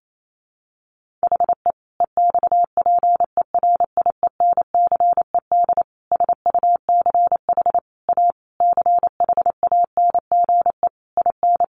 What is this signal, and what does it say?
Morse code, heard as a keyed tone, about 28 wpm, 700 Hz. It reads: HE EXPERIENCED SUCH A CHANGE IN
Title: Sentences from Top 1100 Words 28wpm